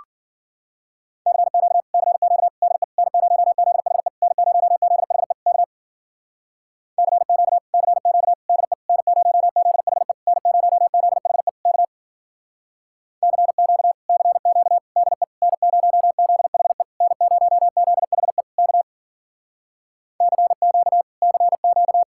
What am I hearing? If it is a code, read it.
Morse code, heard as a keyed tone, about 52 wpm, 700 Hz. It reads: CQ CQ DE N0ZHE N0ZHE K CQ CQ DE N0ZHE N0ZHE K CQ CQ DE N0ZHE N0ZHE K CQ CQ